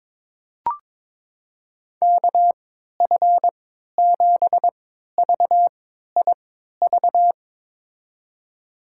Morse code, heard as a keyed tone, about 22 wpm, spaced as for 13 wpm, 700 Hz. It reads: KF7VIV
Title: USA Call Signs - 3x Character Spacing - 22wpm